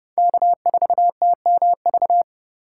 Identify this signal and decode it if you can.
Morse code, heard as a keyed tone, about 30 wpm, 700 Hz. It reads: K4TMV
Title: USA Call Signs - 5 Characters - 3x Character Spacing - 30wpm